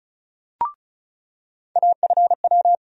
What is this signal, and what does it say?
Morse code, heard as a keyed tone, about 34 wpm, 700 Hz. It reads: AFW
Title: USA Call Sign Postfixes - 3 Letters 35wpm